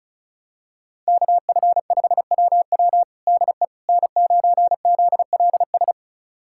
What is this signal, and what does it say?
Morse code, heard as a keyed tone, about 36 wpm, 700 Hz. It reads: KF5WW DE N9ZLS
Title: USA Callsign DE Callsign - 5 Characters - 35wpm